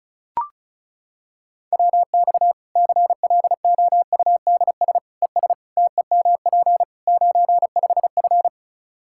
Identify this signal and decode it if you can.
Morse code, heard as a keyed tone, about 34 wpm, 700 Hz. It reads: WX CLOUDS ES TEMP 95F
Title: Practice QSO - Exchange 2 - Info - 35wpm